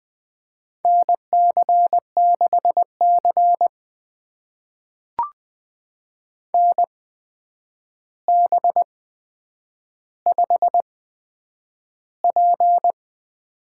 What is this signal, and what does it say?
Morse code, heard as a keyed tone, about 20 wpm, spaced as for 9 wpm, 700 Hz. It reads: NC6C NB5P
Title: USA Call Signs - 4 Characters - 8x Character Spacing - 20wpm